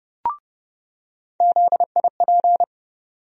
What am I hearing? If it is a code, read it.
Morse code, heard as a keyed tone, about 31 wpm, 700 Hz. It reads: ZIP
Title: USA Call Sign Postfixes - 3 Letters 30wpm